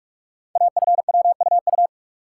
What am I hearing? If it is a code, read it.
Morse code, heard as a keyed tone, about 44 wpm, 700 Hz. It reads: AFWUU
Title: Code Groups - 5 Letters - 45wpm